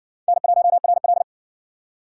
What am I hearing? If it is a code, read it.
Morse code, heard as a keyed tone, about 60 wpm, 700 Hz. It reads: N1RL